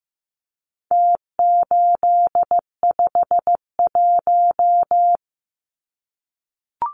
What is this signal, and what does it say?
Morse code, heard as a keyed tone, about 15 wpm, 700 Hz. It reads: T851
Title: Numbers - Contest Style - 4-Digit 15wpm